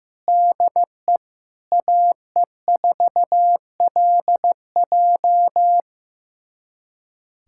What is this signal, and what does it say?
Morse code, heard as a keyed tone, about 15 wpm, 700 Hz. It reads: DE AE4LJ
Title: USA Callsign DE Callsign - 5 Characters - 15wpm